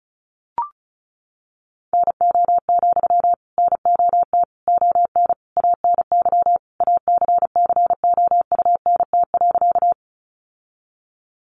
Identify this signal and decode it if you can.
Morse code, heard as a keyed tone, about 34 wpm, 700 Hz. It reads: NO, NOT ON ANY ACCOUNT.
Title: Sentences from Top 900 Words 35wpm